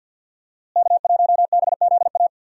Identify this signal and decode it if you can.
Morse code, heard as a keyed tone, about 51 wpm, 700 Hz. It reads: K1BZA
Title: USA Call Signs - 5 Characters - Repeated 3x - 50wpm